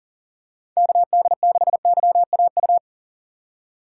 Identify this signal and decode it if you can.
Morse code, heard as a keyed tone, about 40 wpm, 700 Hz. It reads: KD6YAU
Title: USA Call Signs - 6 Characters - 3x Character Spacing - 40wpm